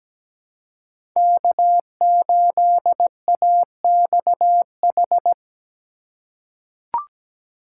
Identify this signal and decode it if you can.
Morse code, heard as a keyed tone, about 17 wpm, 700 Hz. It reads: K8AXH
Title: USA Call Signs - 5 Characters - 3x Character Spacing - 17wpm